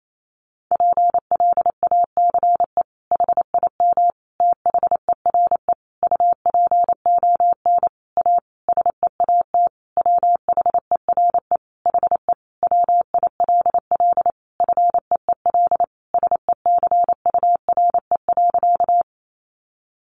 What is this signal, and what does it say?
Morse code, heard as a keyed tone, about 28 wpm, 700 Hz. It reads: PLACE HIM THERE UPON A SEAT WHERE HE WILL FEEL SECURE.
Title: Sentences from Top 1500 Words 28wpm